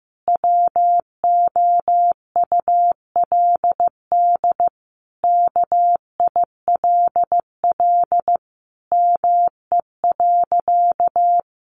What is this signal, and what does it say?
Morse code, heard as a keyed tone, about 15 wpm, 700 Hz. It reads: WOULD KILL ME.
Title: Sentences from Top 1400 Words 15wpm